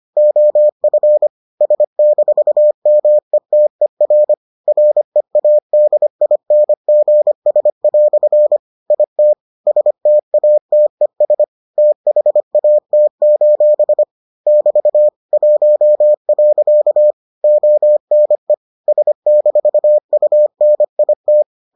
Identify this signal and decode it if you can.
Morse code, heard as a keyed tone, about 25 wpm, 600 Hz. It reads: OF S-METER READINGS" IT STATES THAT: <BT> 1. ONE S-UNIT